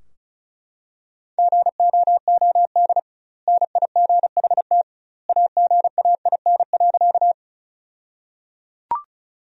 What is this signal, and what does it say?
Morse code, heard as a keyed tone, about 36 wpm, 700 Hz. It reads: GOOD NIGHT AGAIN.
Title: 2x Word-Spacing - Sentences from Top 300 Words - 35wpm